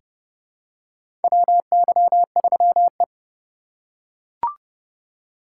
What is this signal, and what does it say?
Morse code, heard as a keyed tone, about 30 wpm, 700 Hz. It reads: WY3E
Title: USA Call Signs - 4 Characters - 3x Character Spacing - 30wpm